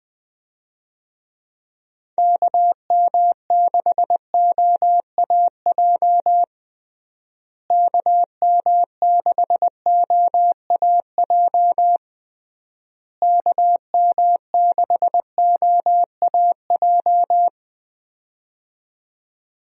Morse code, heard as a keyed tone, about 20 wpm, 700 Hz. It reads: KM6OAJ KM6OAJ KM6OAJ
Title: USA Call Signs - 6 Characters - Repeated 3x - 20wpm